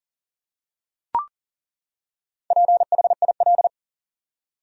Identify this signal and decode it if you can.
Morse code, heard as a keyed tone, about 40 wpm, 700 Hz. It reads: PHIL
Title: USA Common Names - Max 5 Letters - 40wpm